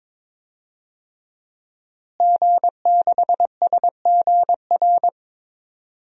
Morse code, heard as a keyed tone, about 22 wpm, 700 Hz. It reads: G6SGR
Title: DXCC-100 Call Signs - 5 Characters - Repeated 3x - 22wpm